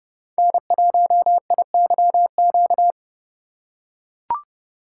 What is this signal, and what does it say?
Morse code, heard as a keyed tone, about 31 wpm, 700 Hz. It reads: N1IYQ
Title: USA Call Signs - 5 Characters - 3x Character Spacing - 30wpm